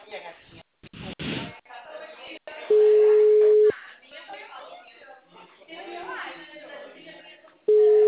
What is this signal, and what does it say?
Ringback tone (Europe, 425 Hz), with speech.